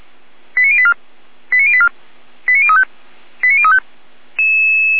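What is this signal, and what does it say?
Five-tone selective call (ZVEI), heard as quick runs of five tones, tones of about 70 ms. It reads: 89084 89084 89035 89035
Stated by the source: Five-tone ZVEI selective call, code 89092, received by SDR, from fire brigade Feuerwehr Orlamünde.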